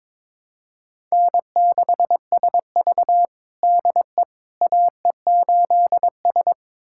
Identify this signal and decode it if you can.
Morse code, heard as a keyed tone, about 22 wpm, 700 Hz. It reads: N6SV DE AE8S
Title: USA Callsign DE Callsign - 4 Characters - 22wpm